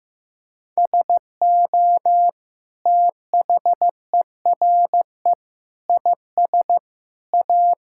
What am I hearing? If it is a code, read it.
Morse code, heard as a keyed tone, about 16 wpm, 700 Hz. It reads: SO THERE IS A
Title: Sentences from Top 1600 Words 15wpm